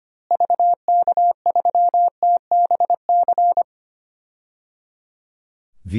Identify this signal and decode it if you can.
Morse code, heard as a keyed tone, about 25 wpm, 700 Hz. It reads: VK3TBC
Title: DXCC-100 Call Signs - 6 Characters - Repeated 3x - 25wpm